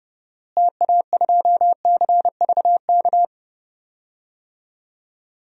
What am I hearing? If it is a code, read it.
Morse code, heard as a keyed tone, about 30 wpm, 700 Hz. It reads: TA2CVK
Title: DXCC-100 Call Signs - 6 Characters - Repeated 3x - 30wpm